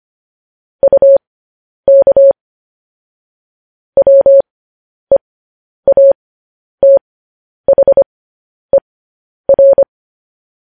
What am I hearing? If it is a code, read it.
Morse code, heard as a keyed tone, about 25 wpm, spaced as for 10 wpm, 550 Hz. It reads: UK WEATHER
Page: news headlines in morse code at 10 wpm